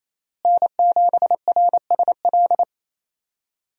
Morse code, heard as a keyed tone, about 28 wpm, 700 Hz. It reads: N7RSL
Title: USA Call Signs - 5 Characters - Repeated 3x - 28wpm